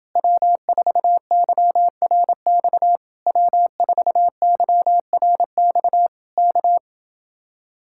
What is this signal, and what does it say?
Morse code, heard as a keyed tone, about 30 wpm, 700 Hz. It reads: W4YRX W4YRX K